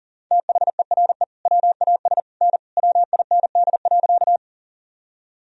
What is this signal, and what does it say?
Morse code, heard as a keyed tone, about 40 wpm, 700 Hz. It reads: THERE WAS N WIND.
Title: Sentences from Top 400 Words and Abbreviations 40wpm